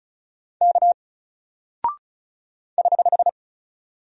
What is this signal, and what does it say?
Morse code, heard as a keyed tone, about 36 wpm, 700 Hz. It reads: K <HH>
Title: Single Character (Letter:Number:Punctuation:Prosign) Alt 35wpm